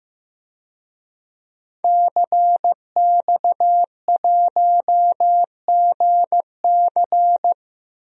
Morse code, heard as a keyed tone, about 15 wpm, 700 Hz. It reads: CX1GC
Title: DXCC-100 Call Signs - 5 Characters - Repeated 3x - 15wpm